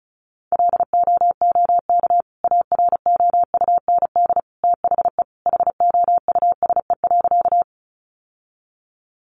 Morse code, heard as a keyed tone, about 34 wpm, 700 Hz. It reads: LOOK AROUND THE HOUSE.
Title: Sentences from Top 600 Words 35wpm